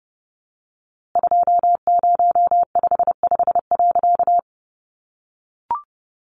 Morse code, heard as a keyed tone, about 30 wpm, 700 Hz. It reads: 2055.
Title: A Sound Of Thunder 30wpm